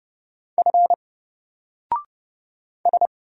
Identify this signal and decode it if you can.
Morse code, heard as a keyed tone, about 30 wpm, 700 Hz. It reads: F S
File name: Single-Letter-Number-30wpm